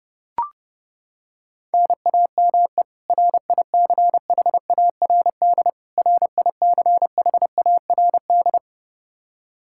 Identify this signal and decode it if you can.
Morse code, heard as a keyed tone, about 30 wpm, 700 Hz. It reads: NAME RICHARD RICHARD
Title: Practice QSO - Exchange 1 - Beginning - 30wpm